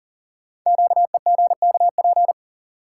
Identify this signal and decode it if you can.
Morse code, heard as a keyed tone, about 40 wpm, 700 Hz. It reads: QEGKP
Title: Code Groups - 5 Letters - 40wpm